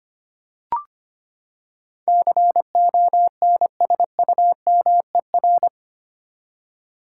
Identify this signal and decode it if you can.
Morse code, heard as a keyed tone, about 25 wpm, 700 Hz. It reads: CONSUMER
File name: Eight Letter Words - 25wpm